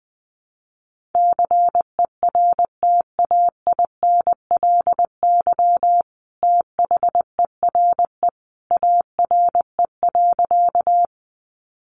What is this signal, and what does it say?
Morse code, heard as a keyed tone, about 20 wpm, 700 Hz. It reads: CERTAINLY THERE ARE.
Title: Sentences from Top 900 Words 20wpm